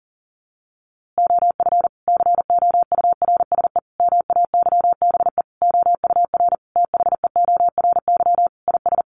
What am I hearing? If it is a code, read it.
Morse code, heard as a keyed tone, about 40 wpm, 700 Hz. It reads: OF COURSE MAYBE OUR THEORY IS
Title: A Sound Of Thunder 40wpm